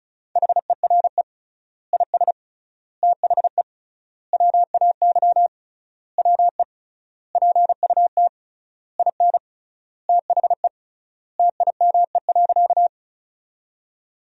Morse code, heard as a keyed tone, about 36 wpm, 700 Hz. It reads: HERE IS THE WAY WE PUT IN THE TIME.
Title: 3x Word-Spacing - Sentences from Top 200 Words - 35wpm